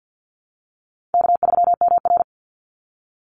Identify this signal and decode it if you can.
Morse code, heard as a keyed tone, about 48 wpm, 700 Hz. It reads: K3MR
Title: Easy Call Signs Repeated 3x - 50wpm